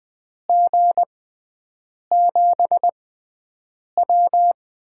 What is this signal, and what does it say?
Morse code, heard as a keyed tone, about 20 wpm, spaced as for 7 wpm, 700 Hz. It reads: G7W
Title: USA Call Signs - 6 Characters - 6x Character Spacing - 20wpm